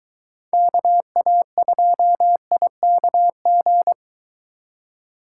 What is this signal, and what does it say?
Morse code, heard as a keyed tone, about 22 wpm, 700 Hz. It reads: KA2IKG